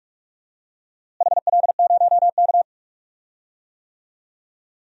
Morse code, heard as a keyed tone, about 46 wpm, 700 Hz. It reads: HL0K
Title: DXCC-100 Call Signs - 4 Characters - Repeated 3x - 45wpm